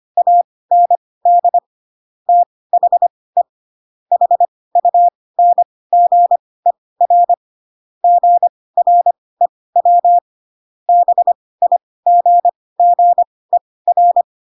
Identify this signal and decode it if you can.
Morse code, heard as a keyed tone, about 25 wpm, spaced as for 18 wpm, 700 Hz. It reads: AND THE HUNGER GREW BIGGER